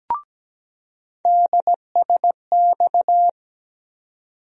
Morse code, heard as a keyed tone, about 17 wpm, 700 Hz. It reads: DSX
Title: USA Call Sign Postfixes - 3 Letters 17wpm